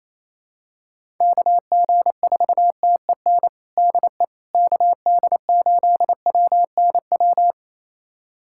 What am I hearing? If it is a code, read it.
Morse code, heard as a keyed tone, about 27 wpm, 700 Hz. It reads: KG4TEN DE KD8WNW
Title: USA Callsign DE Callsign - 6 Characters - 28wpm